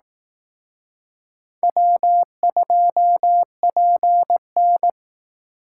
Morse code, heard as a keyed tone, about 18 wpm, 700 Hz. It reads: W2PN
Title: Speed Racing - USA 1x2 Call Signs 15wpm